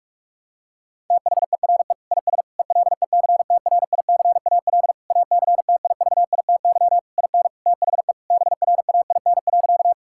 Morse code, heard as a keyed tone, about 41 wpm, 690 Hz. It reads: THERE IS ELECTRICAL ACTIVITY IN THE BRAIN.
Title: Sentences from Top 1000 Words - Review Part 6 - 45wpm